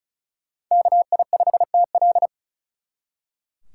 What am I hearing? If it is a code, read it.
Morse code, heard as a keyed tone, about 45 wpm, 700 Hz. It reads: KI5TL